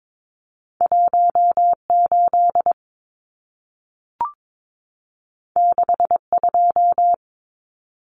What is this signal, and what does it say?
Morse code, heard as a keyed tone, about 22 wpm, 700 Hz. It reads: 18 62
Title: Numbers - 2-Digit 22wpm